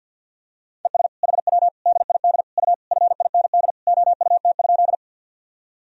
Morse code, heard as a keyed tone, about 50 wpm, 700 Hz. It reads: ES HW DID U FIND OUT?
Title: Sentences from Top 200 Words and Abbreviations 50wpm